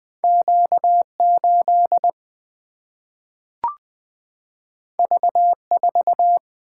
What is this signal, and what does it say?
Morse code, heard as a keyed tone, about 20 wpm, 700 Hz. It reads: Q8 V4
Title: Single-Letter Single-Number 20wpm